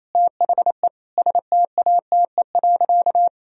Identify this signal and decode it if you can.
Morse code, heard as a keyed tone, about 28 wpm, 700 Hz. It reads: THE STATE.
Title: Sentences from Top 1000 Words - Review Part 4 - 28wpm